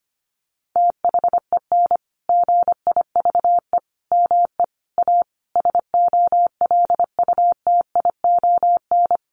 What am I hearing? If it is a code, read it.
Morse code, heard as a keyed tone, about 25 wpm, 700 Hz. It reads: THEN GIVE ME A SOLUTION
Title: Sentences from Top 1300 Words 25wpm